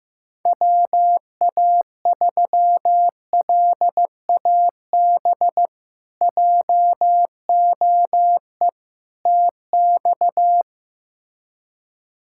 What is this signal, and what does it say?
Morse code, heard as a keyed tone, about 15 wpm, 700 Hz. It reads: WA3LAB JOE TX
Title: K1USN SST QSO Practice - 15wpm